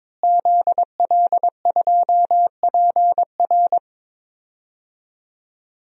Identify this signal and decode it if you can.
Morse code, heard as a keyed tone, about 21 wpm, 700 Hz. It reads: ZL2PR
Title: DXCC-100 Call Signs - 5 Characters - Repeated 3x - 22wpm